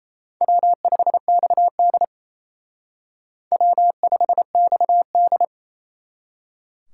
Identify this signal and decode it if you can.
Morse code, heard as a keyed tone, about 35 wpm, 700 Hz. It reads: W5XD W5XD